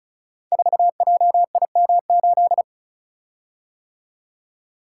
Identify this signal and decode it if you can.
Morse code, heard as a keyed tone, about 36 wpm, 690 Hz. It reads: 4JIM8
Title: Code Groups - 5 Letter:Numbers - 35wpm